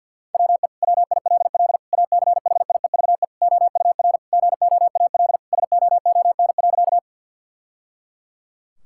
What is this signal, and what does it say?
Morse code, heard as a keyed tone, about 50 wpm, 700 Hz. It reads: WE WILL ACHIEVE OUR GOAL SOON.
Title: Sentences from Top 1000 Words - Review Part 6 - 50wpm